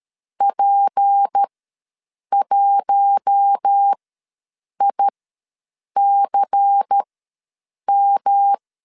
Morse code, heard as a keyed tone, about 13 wpm, spaced as for 7 wpm, 800 Hz. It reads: P1ICM
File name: LEZIONI / 6 PKX0 / 6 P ws 12wpm 3117